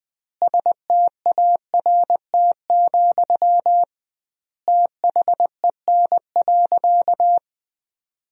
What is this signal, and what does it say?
Morse code, heard as a keyed tone, about 20 wpm, 700 Hz. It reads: START, THEN.
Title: 2x Word-Spacing - Sentences from Top 300 Words - 20wpm